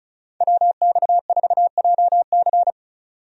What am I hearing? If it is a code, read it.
Morse code, heard as a keyed tone, about 36 wpm, 700 Hz. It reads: WX4JC
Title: USA Call Signs - 5 Characters - 35wpm